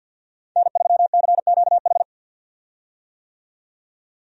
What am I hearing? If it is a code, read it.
Morse code, heard as a keyed tone, about 51 wpm, 700 Hz. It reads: N3CQH
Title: USA Call Signs - 5 Characters - 50wpm